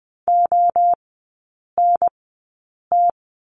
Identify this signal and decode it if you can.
Morse code, heard as a keyed tone, about 20 wpm, 700 Hz. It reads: O N T
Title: 029 Lesson 6e - Letter i - Three Characters 20wpm